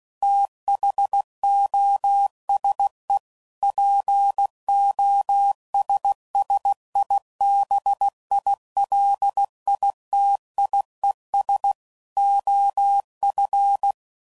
Morse code, heard as a keyed tone, about 16 wpm, 800 Hz. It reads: THOSE POSSIBILITIES OF
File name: CW Code Groups @15WPM / Technical Subjects / OR Gate